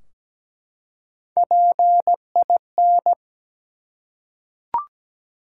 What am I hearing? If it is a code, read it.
Morse code, heard as a keyed tone, about 17 wpm, 700 Hz. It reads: PIN
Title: Three Letter Words - 17wpm